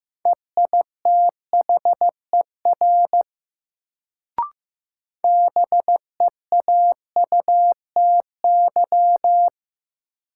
Morse code, heard as a keyed tone, about 16 wpm, 700 Hz. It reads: EITHER BEAUTY
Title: Sentences from Top 600 Words - New Word Drill - 15wpm